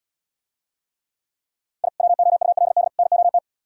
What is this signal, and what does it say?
Morse code, heard as a keyed tone, about 76 wpm, 700 Hz. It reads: I WOULD NOT